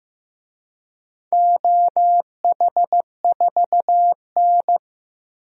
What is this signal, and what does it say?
Morse code, heard as a keyed tone, about 15 wpm, 700 Hz. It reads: OH4N
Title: DXCC-100 Call Signs - 4 Characters - 15wpm